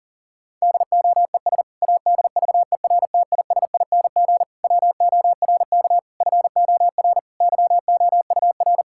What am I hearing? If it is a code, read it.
Morse code, heard as a keyed tone, about 40 wpm, 700 Hz. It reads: DOES ADVERTISING WORK FOR YOUR
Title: Sentences from Top 1000 Words - Review Part 6 - 40wpm